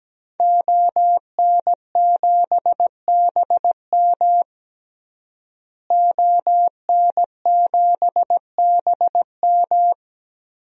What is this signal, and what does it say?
Morse code, heard as a keyed tone, about 17 wpm, 700 Hz. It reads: ON7BM ON7BM
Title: DXCC-100 Call Signs - 5 Characters - Repeated 3x - 17wpm